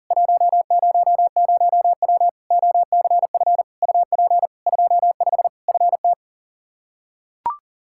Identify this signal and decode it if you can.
Morse code, heard as a keyed tone, about 40 wpm, 700 Hz. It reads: 100W OCF UP 25 FT
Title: Practice QSO - Exchange 2 - Info - 40wpm